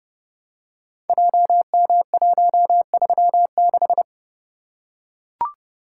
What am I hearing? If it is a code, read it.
Morse code, heard as a keyed tone, about 29 wpm, 700 Hz. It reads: JM136
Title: Code Groups - 5 Letter:Numbers - 30wpm